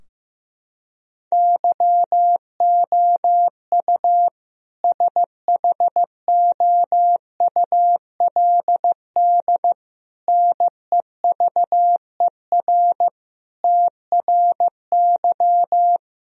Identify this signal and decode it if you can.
Morse code, heard as a keyed tone, about 15 wpm, 700 Hz. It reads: YOU SHOULD NEVER TRY
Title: Sentences from Top 1000 Words - Review Part 3 - 15wpm